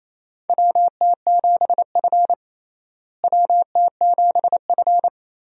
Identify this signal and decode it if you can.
Morse code, heard as a keyed tone, about 28 wpm, 700 Hz. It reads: WT7F WT7F
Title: USA Call Signs - 4 Characters - Repeated 3x - 28wpm